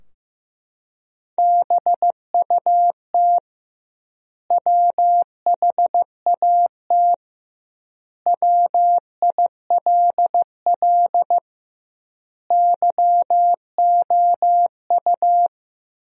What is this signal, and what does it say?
Morse code, heard as a keyed tone, about 15 wpm, 700 Hz. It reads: BUT WHAT WILL YOU
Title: 2x Word-Spacing - Sentences from Top 100 Words - 15wpm